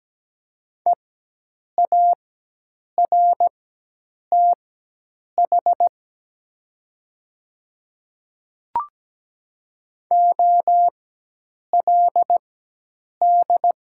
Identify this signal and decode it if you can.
Morse code, heard as a keyed tone, about 17 wpm, spaced as for 8 wpm, 690 Hz. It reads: EARTH OLD
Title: 4x Character Spacing - Top 200 Words - 17wpm